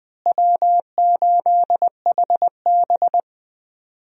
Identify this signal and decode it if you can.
Morse code, heard as a keyed tone, about 21 wpm, 700 Hz. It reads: W8HB
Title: USA Call Signs - 4 Characters - Repeated 3x - 20wpm